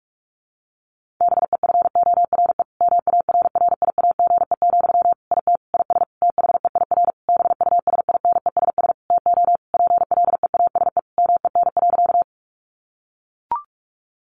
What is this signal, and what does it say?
Morse code, heard as a keyed tone, about 44 wpm, 700 Hz. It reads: BEFORE MARRIAGE, IT IS THEIR BUSINESS TO PLEASE MEN.
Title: Sentences from Top 1500 Words 45wpm